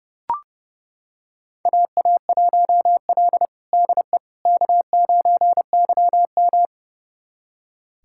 Morse code, heard as a keyed tone, about 30 wpm, 700 Hz. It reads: AA1L DE K9YM
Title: USA Callsign DE Callsign - 4 Characters - 30wpm